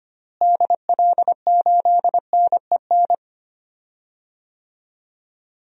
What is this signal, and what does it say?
Morse code, heard as a keyed tone, about 25 wpm, 700 Hz. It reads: DL8NEN
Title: DXCC-100 Call Signs - 6 Characters - Repeated 3x - 25wpm